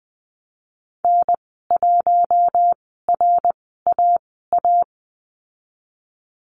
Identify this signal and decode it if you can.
Morse code, heard as a keyed tone, about 20 wpm, spaced as for 15 wpm, 700 Hz. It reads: N1RAA
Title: 088 Lesson 15g - Number 5 - Call Signs 20wpm